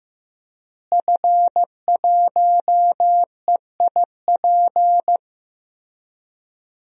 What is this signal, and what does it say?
Morse code, heard as a keyed tone, about 15 wpm, 700 Hz. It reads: F1EIP
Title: DXCC-100 Call Signs - 5 Characters - Repeated 3x - 15wpm